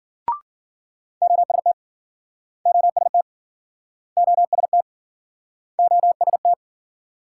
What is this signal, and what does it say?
Morse code, heard as a keyed tone, about 57 wpm, 700 Hz. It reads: OST OST OST OST